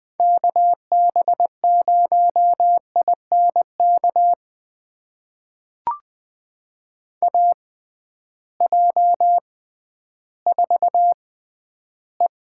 Morse code, heard as a keyed tone, about 20 wpm, spaced as for 12 wpm, 700 Hz. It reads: KB0INK AJ4E
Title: USA Call Signs - 6x Character Spacing - 20wpm